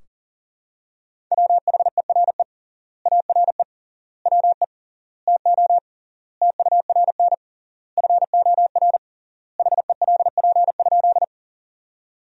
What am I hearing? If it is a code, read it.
Morse code, heard as a keyed tone, about 40 wpm, 700 Hz. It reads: WHERE ARE WE TO TURN FOR HELP?
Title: 3x Word-Spacing - Sentences from Top 200 Words - 40wpm